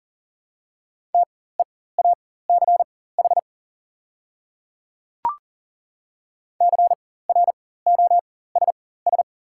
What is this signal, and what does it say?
Morse code, heard as a keyed tone, about 40 wpm, spaced as for 19 wpm, 690 Hz. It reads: TEACH CROSS
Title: 4x Character Spacing - Top 500 Words - 40wpm